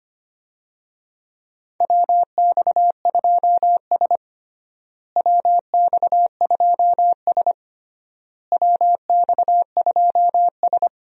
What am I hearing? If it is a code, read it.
Morse code, heard as a keyed tone, about 25 wpm, 700 Hz. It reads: WX2S WX2S WX2S
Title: USA Call Signs - 4 Characters - Repeated 3x - 25wpm